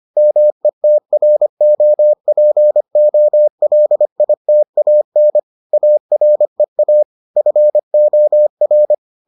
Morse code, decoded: METROPOLITAN AREA FOR

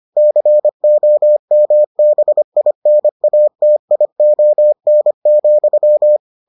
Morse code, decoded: COMBINATION,

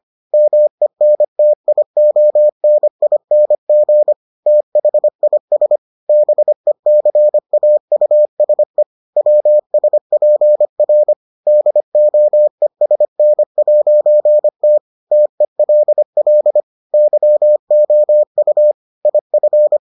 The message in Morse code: MENTIONING THIS BECAUSE WSPR DOESN'T TELL YOU IF

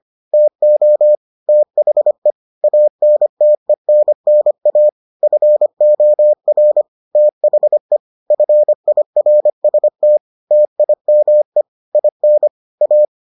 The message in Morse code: TO THE ANTENNA FOR THE FIRST TIME IN A

